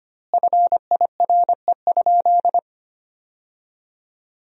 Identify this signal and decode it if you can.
Morse code, heard as a keyed tone, about 25 wpm, 700 Hz. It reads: FIRE?
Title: 2x Word-Spacing - Sentences from Top 400 Words - 25wpm